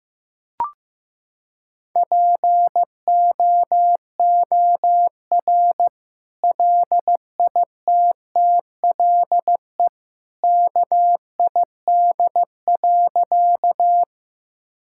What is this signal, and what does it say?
Morse code, heard as a keyed tone, about 16 wpm, 700 Hz. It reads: POOR LITTLE KID.
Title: Sentences from Top 1600 Words 15wpm